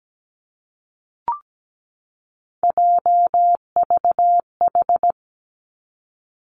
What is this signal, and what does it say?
Morse code, heard as a keyed tone, about 17 wpm, 700 Hz. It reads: JVH
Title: Three Challenging Letter-Number Combinations 17wpm